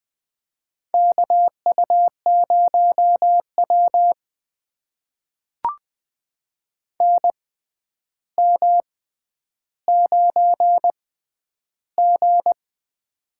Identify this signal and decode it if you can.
Morse code, heard as a keyed tone, about 20 wpm, spaced as for 10 wpm, 700 Hz. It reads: KU0W NM9G
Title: USA Call Signs - 4 Characters - 6x Character Spacing - 20wpm